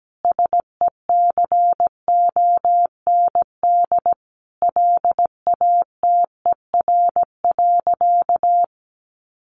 Morse code, heard as a keyed tone, about 17 wpm, 700 Hz. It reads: SECOND LATER.
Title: Sentences from Top 900 Words 17wpm